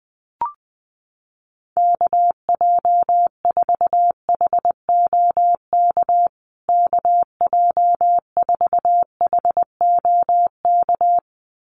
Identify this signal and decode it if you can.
Morse code, heard as a keyed tone, about 19 wpm, 700 Hz. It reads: KJ4HOK KJ4HOK
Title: USA Call Signs Repeated 2x 20wpm